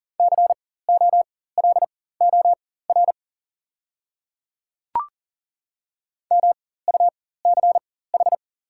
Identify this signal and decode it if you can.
Morse code, heard as a keyed tone, about 41 wpm, spaced as for 19 wpm, 700 Hz. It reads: COLOR MUCH
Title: 4x Character Spacing - Top 300 Words - 40wpm